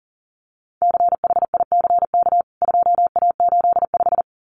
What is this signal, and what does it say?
Morse code, heard as a keyed tone, about 40 wpm, 700 Hz. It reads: CHICK 2A85
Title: CWT Exchange 40wpm